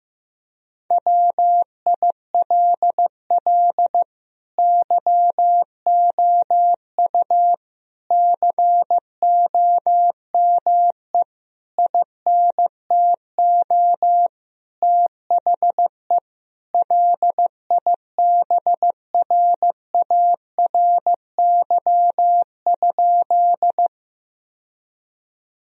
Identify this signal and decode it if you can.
Morse code, heard as a keyed tone, about 15 wpm, 700 Hz. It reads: WILL YOU COME INTO THE LIBRARY?
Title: Sentences from Top 1000 Words - Review Part 4 - 15wpm